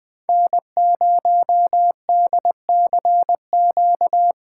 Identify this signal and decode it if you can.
Morse code, heard as a keyed tone, about 20 wpm, 700 Hz. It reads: N0DCQ
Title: USA Call Signs - 5 Characters - 6x Character Spacing - 20wpm